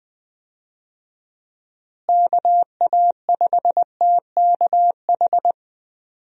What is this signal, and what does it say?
Morse code, heard as a keyed tone, about 20 wpm, 700 Hz. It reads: KA5TKH
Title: USA Call Signs - 6 Characters - Repeated 2x - 20wpm